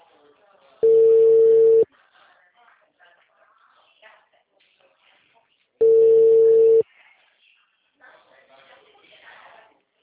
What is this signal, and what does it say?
Ringback tone (China, 450 Hz), with speech.
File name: out- -2019- - - .29